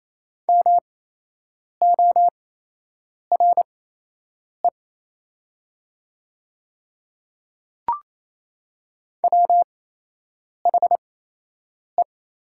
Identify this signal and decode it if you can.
Morse code, heard as a keyed tone, about 28 wpm, spaced as for 8 wpm, 700 Hz. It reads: MORE WHE
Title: 8x Character Spacing - Top 100 Words - 28wpm